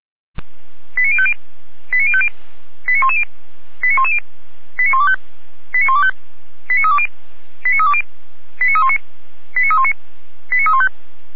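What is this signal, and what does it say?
Five-tone selective call (ZVEI), heard as quick runs of five tones, tones of about 80 ms. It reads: 89050 89050 89110 89110 89125 89125 89320 89320 89319 89319 89315
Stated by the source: Five-tone ZVEI selective call 89294 received by SDR.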